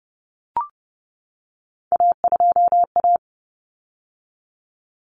Morse code, heard as a keyed tone, about 30 wpm, 700 Hz. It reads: A2A
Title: Common Cut Numbers - 3-Digit 30wpm